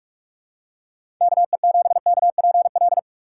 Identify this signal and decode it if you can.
Morse code, heard as a keyed tone, about 47 wpm, 700 Hz. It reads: KE7KPL